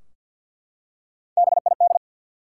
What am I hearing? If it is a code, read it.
Morse code, heard as a keyed tone, about 48 wpm, 700 Hz. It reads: BID
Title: Three Letter Words - 50wpm